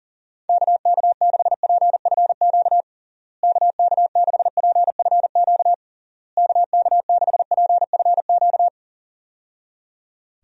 Morse code, heard as a keyed tone, about 37 wpm, 700 Hz. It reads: KK6PFQ KK6PFQ KK6PFQ
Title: USA Call Signs - 6 Characters - Repeated 3x - 40wpm